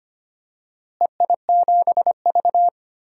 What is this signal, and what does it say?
Morse code, heard as a keyed tone, about 25 wpm, 700 Hz. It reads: EI7V
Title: DXCC-100 Call Signs - 4 Characters - Repeated 2x - 25wpm